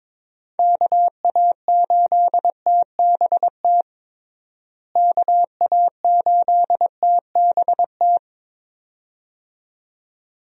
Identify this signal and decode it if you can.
Morse code, heard as a keyed tone, about 22 wpm, 700 Hz. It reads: KA8TBT KA8TBT
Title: USA Call Signs - 6 Characters - Repeated 2x - 22wpm